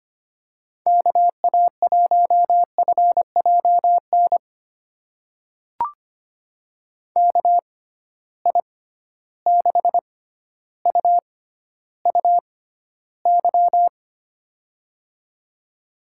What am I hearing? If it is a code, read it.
Morse code, heard as a keyed tone, about 25 wpm, spaced as for 13 wpm, 700 Hz. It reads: KA1FJN KI6UUY
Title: USA Call Signs - 6 Characters - 6x Character Spacing - 25wpm